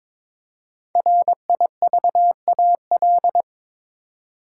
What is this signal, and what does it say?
Morse code, heard as a keyed tone, about 22 wpm, 700 Hz. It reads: RIVAL
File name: 5 Letter Words - First 3 Letters Given - 22wpm